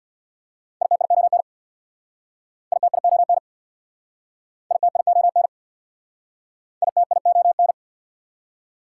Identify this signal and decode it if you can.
Morse code, heard as a keyed tone, about 67 wpm, 690 Hz. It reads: ITION ITION ITION ITION